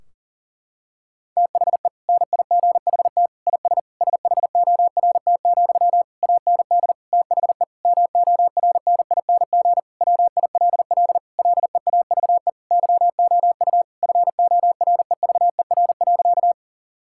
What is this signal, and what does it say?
Morse code, heard as a keyed tone, about 40 wpm, 700 Hz. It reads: THE NIGHT IS SHORT, AND THE MORNING WILL LEAVE YOU FOREVER.
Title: Sentences from Top 1600 Words 40wpm